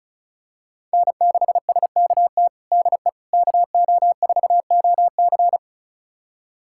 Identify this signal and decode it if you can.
Morse code, heard as a keyed tone, about 36 wpm, 700 Hz. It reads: N6SKT DE KO4OC
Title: USA Callsign DE Callsign - 5 Characters - 35wpm